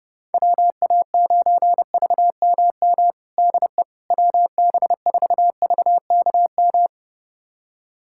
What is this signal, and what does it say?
Morse code, heard as a keyed tone, about 30 wpm, 700 Hz. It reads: WA9VMM DE WB4VKM
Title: USA Callsign DE Callsign - 6 Characters - 30wpm